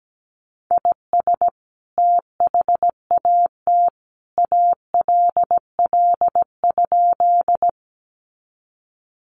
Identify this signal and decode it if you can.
Morse code, heard as a keyed tone, about 17 wpm, 700 Hz. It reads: IS THAT ALL?
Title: Sentences from Top 100 Words 17wpm